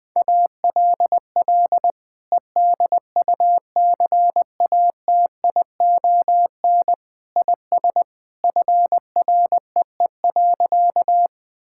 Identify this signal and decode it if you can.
Morse code, heard as a keyed tone, about 20 wpm, 700 Hz. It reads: ALL EDUCATION IS FREE.
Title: Sentences from Top 1000 Words - Review Part 6 - 20wpm